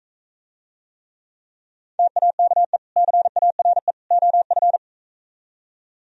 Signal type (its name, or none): Morse code